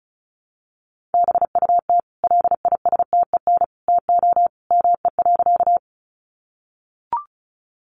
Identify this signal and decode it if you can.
Morse code, heard as a keyed tone, about 34 wpm, 700 Hz. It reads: BUT LISTEN TO ME.
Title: Sentences from Top 400 Words 35wpm